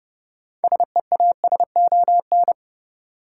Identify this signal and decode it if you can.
Morse code, heard as a keyed tone, about 30 wpm, 700 Hz. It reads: SEASON
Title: Sentences from Top 1400 Words - New Word Drill - 30wpm